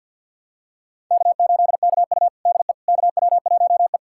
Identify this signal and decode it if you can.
Morse code, heard as a keyed tone, about 49 wpm, 700 Hz. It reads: K8KU DE KW1E
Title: USA Callsign DE Callsign - 4 Characters - 50wpm